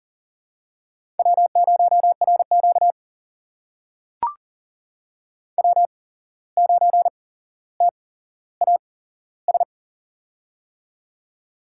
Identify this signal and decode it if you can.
Morse code, heard as a keyed tone, about 40 wpm, spaced as for 16 wpm, 700 Hz. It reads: W0RQ W9TAS
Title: USA Call Signs - 8x Character Spacing - 40wpm